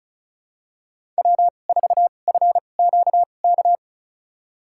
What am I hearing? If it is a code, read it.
Morse code, heard as a keyed tone, about 36 wpm, spaced as for 25 wpm, 700 Hz. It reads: W4FQK
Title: USA Call Signs - 5 Characters - 2x Character Spacing - 35wpm